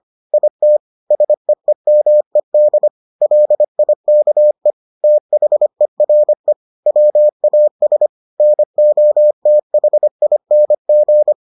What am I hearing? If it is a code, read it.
Morse code, heard as a keyed tone, about 25 wpm, 600 Hz. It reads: IT SEEMED LIKE THERE WAS NOTHING